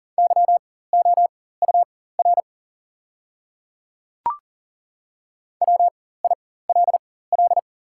Morse code, heard as a keyed tone, about 38 wpm, spaced as for 18 wpm, 700 Hz. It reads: YOUR WILL
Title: 4x Character Spacing - Top 100 Words - 40wpm